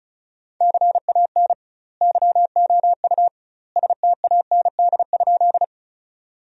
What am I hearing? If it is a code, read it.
Morse code, heard as a keyed tone, about 36 wpm, 700 Hz. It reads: CAN YOU STAND?
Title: 2x Word-Spacing - Sentences from Top 300 Words - 35wpm